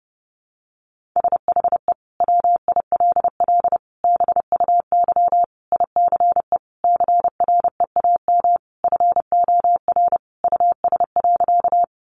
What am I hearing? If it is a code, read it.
Morse code, heard as a keyed tone, about 30 wpm, 700 Hz. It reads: SHE WILL BUY ICE CREAM FOR US.
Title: Sentences from Top 1500 Words 30wpm